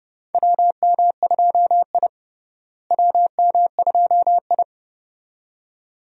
Morse code, heard as a keyed tone, about 30 wpm, 700 Hz. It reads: WM2I WM2I
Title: USA Call Signs - 4 Characters - Repeated 2x - 30wpm